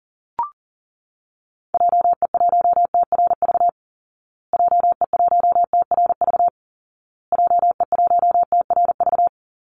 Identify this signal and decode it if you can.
Morse code, heard as a keyed tone, about 40 wpm, 700 Hz. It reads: JE1TRV JE1TRV JE1TRV
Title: Call Signs Repeated 2x - 40wpm